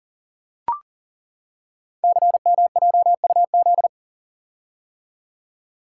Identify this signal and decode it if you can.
Morse code, heard as a keyed tone, about 41 wpm, 700 Hz. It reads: CMJUZ
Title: Code Groups - 5 Letters - 40wpm